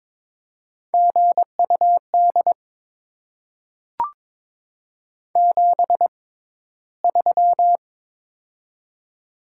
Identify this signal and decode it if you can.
Morse code, heard as a keyed tone, about 22 wpm, spaced as for 13 wpm, 690 Hz. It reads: GUD 73
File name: Top 20 QSO Elements - 6x Character Spacing - 22wpm